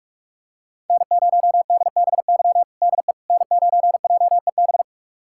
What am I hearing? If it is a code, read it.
Morse code, heard as a keyed tone, about 46 wpm, 700 Hz. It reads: N0DBY DE N9JEB
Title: USA Callsign DE Callsign 45wpm